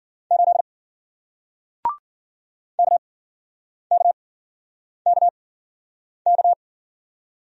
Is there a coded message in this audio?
Yes: Morse code from 0.3 to 6.5 s, heard as a keyed tone, about 52 wpm, spaced as for 9 wpm, 700 Hz. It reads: 8 KKKK